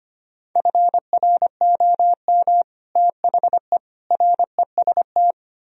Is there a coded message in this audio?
Yes: Morse code from 0.6 to 5.3 s, heard as a keyed tone, about 32 wpm, 700 Hz. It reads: FROM THE REST